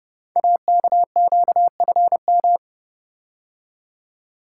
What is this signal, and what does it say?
Morse code, heard as a keyed tone, about 32 wpm, 700 Hz. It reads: AKQFM
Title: Code Groups - 5 Letter:Numbers - 30wpm